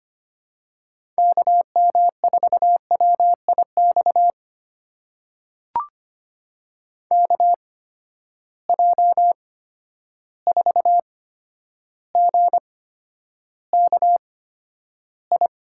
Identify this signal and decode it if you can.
Morse code, heard as a keyed tone, about 25 wpm, spaced as for 11 wpm, 700 Hz. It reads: KM4WIX KJ4GKI
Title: USA Call Signs - 6 Characters - 8x Character Spacing - 25wpm